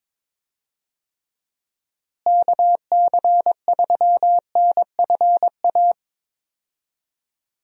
Morse code, heard as a keyed tone, about 22 wpm, 700 Hz. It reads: KC3NFA
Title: USA Call Signs - 6 Characters - Repeated 2x - 22wpm